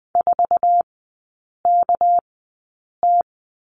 Morse code, heard as a keyed tone, about 20 wpm, spaced as for 8 wpm, 690 Hz. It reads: 4KT
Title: 205 Lesson 33e - Letter j - Three Characters 20wpm